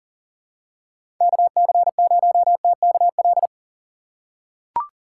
Morse code, heard as a keyed tone, about 39 wpm, 700 Hz. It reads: KC0TKL
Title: USA Call Signs - 6 Characters - 3x Character Spacing - 40wpm